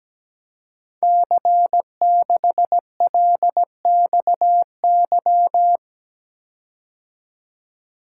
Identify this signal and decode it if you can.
Morse code, heard as a keyed tone, about 17 wpm, 700 Hz. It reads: C6LXY
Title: Code Groups - 5 Letter:Numbers - 17wpm